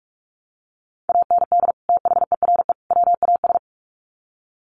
Morse code, heard as a keyed tone, about 44 wpm, 700 Hz. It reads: AND THERE WAS
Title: Sets of 3 Words 45wpm